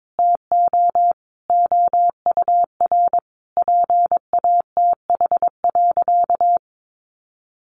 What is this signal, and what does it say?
Morse code, heard as a keyed tone, about 22 wpm, 700 Hz. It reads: TO OUR PATH.
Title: Sentences from Top 1400 Words 22wpm